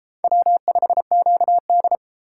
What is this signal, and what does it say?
Morse code, heard as a keyed tone, about 32 wpm, 700 Hz. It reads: W5QD